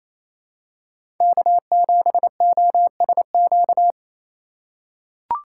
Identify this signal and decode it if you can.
Morse code, heard as a keyed tone, about 28 wpm, 700 Hz. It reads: K7OSQ
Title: USA Call Signs - 5 Characters - 2x Character Spacing - 28wpm